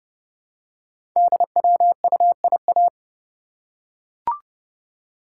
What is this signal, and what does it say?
Morse code, heard as a keyed tone, about 30 wpm, 700 Hz. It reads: DWUIA
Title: Code Groups - 5 Letter:Numbers - 30wpm